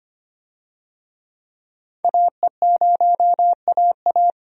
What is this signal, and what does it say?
Morse code, heard as a keyed tone, about 25 wpm, 700 Hz. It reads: AE0AA
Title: USA Call Signs - 5 Characters - 25wpm